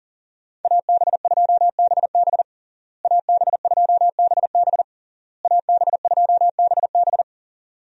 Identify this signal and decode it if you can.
Morse code, heard as a keyed tone, about 40 wpm, 700 Hz. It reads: AB2BB AB2BB AB2BB
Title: USA Call Signs - 5 Characters - Repeated 3x - 40wpm